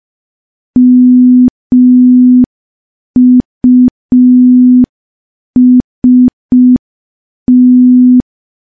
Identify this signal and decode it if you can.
Morse code, heard as a keyed tone, about 5 wpm, 250 Hz. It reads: MUST
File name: morse / morse-training-05WPM